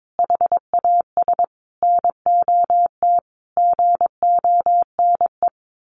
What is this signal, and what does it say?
Morse code, heard as a keyed tone, about 22 wpm, 700 Hz. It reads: HAS NOT GONE
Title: A Sound Of Thunder 22wpm